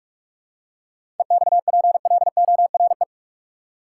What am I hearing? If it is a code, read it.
Morse code, heard as a keyed tone, about 44 wpm, 700 Hz. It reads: EXPLORE
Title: Seven Letter Words - 45wpm